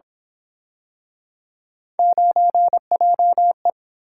Morse code, heard as a keyed tone, about 28 wpm, 700 Hz. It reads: 9JE